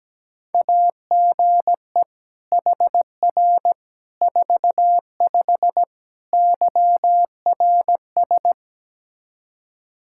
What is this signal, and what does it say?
Morse code, heard as a keyed tone, about 17 wpm, 700 Hz. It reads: AGE HR 45 YRS
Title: K1USN SST QSO Practice 17wpm